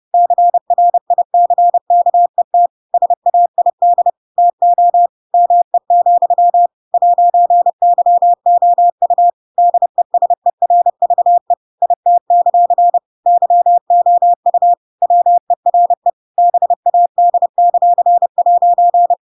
Morse code, heard as a keyed tone, about 30 wpm, 700 Hz. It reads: CRICKET SAID TO ME, 'YOU DESERVE IT; YOU WERE BAD;'
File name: Pinocchio / Pinocchio ch7 30 30 ik1hge